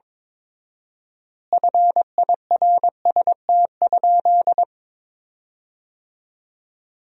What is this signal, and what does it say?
Morse code, heard as a keyed tone, about 22 wpm, 700 Hz. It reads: FIRST?
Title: 4x Word-Spacing - Sentences from Top 100 Words - 22wpm